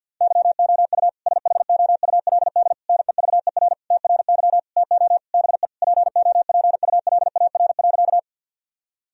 Morse code, heard as a keyed tone, about 50 wpm, 700 Hz. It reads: YOU SHOULD NEVER TRY TO BE POPULAR.
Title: Sentences from Top 1000 Words - Review Part 3 - 50wpm